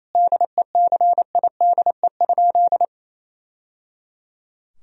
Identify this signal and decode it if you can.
Morse code, heard as a keyed tone, about 28 wpm, 700 Hz. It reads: DECIDE?
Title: Sentences from Top 1000 Words - Review Part 3 - 28wpm